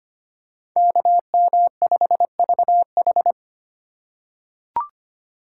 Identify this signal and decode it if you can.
Morse code, heard as a keyed tone, about 25 wpm, 700 Hz. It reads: KM5VH
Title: USA Call Signs - 5 Characters - 2x Character Spacing - 25wpm